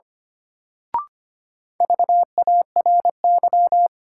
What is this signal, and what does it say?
Morse code, heard as a keyed tone, about 25 wpm, 700 Hz. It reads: VARY
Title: Sentences from Top 1200 Words - New Word Drill Alt - 25wpm